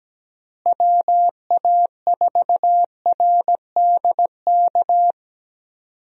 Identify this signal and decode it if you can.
Morse code, heard as a keyed tone, about 17 wpm, 700 Hz. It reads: WA4RDK
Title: USA Call Signs - 6 Characters - Repeated 3x - 17wpm